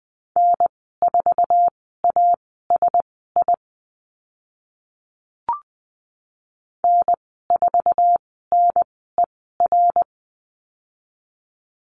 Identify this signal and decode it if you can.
Morse code, heard as a keyed tone, about 20 wpm, spaced as for 14 wpm, 700 Hz. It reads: N4ASI N4NER
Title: 082 Lesson 14g - Number 2 - Call Signs 20wpm